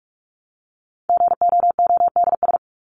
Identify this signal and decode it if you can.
Morse code, heard as a keyed tone, about 44 wpm, 700 Hz. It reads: GOODS
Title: ES:S Suffix - Top 100 Words with Base Words 45wpm